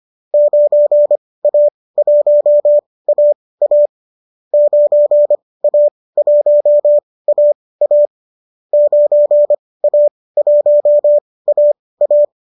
Morse code, heard as a keyed tone, about 25 wpm, spaced as for 18 wpm, 590 Hz. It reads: 9A1AA 9A1AA 9A1AA